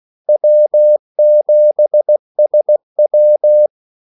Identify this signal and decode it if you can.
Morse code, heard as a keyed tone, about 16 wpm, 600 Hz. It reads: W7SW
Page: Kaldesignaler 16 wpm